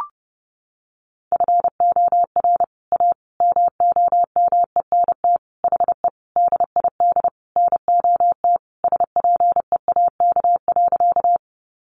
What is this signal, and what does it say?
Morse code, heard as a keyed tone, about 30 wpm, 700 Hz. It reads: FOR A MOMENT HE DID NOT SPEAK.
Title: Sentences from Top 1100 Words 30wpm